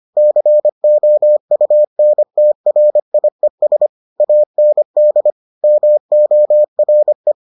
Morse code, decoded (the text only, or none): COUNTRIES AND MORE